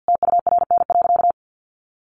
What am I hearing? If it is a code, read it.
Morse code, heard as a keyed tone, about 50 wpm, 700 Hz. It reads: TURN.
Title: Sentences from Top 700 Words 50wpm